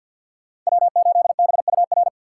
Morse code, heard as a keyed tone, about 53 wpm, 700 Hz. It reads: W8BUR
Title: USA Call Signs - 6x Character Spacing - 50wpm